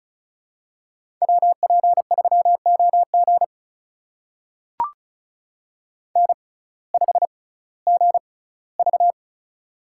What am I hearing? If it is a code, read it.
Morse code, heard as a keyed tone, about 34 wpm, spaced as for 19 wpm, 700 Hz. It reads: WP3OG N5GV
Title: USA Call Signs - 6x Character Spacing - 35wpm